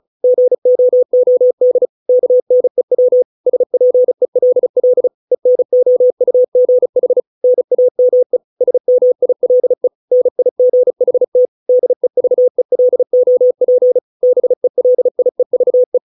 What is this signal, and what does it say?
Morse code, heard as a keyed tone, about 34 wpm, 500 Hz. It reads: GOOD KNEW SPELL ENOUGH NAME SMILE NIGHT DEVELOP BELIEVE